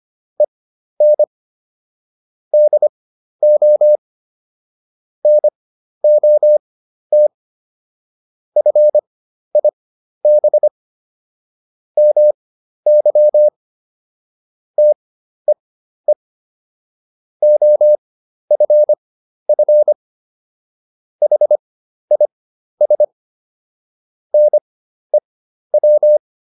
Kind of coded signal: Morse code

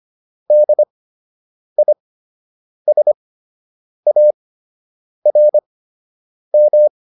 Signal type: Morse code

